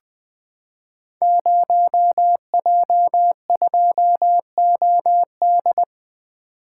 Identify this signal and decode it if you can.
Morse code, heard as a keyed tone, about 20 wpm, 700 Hz. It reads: 0J2OD
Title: Code Groups - 5 Letter:Numbers - 20wpm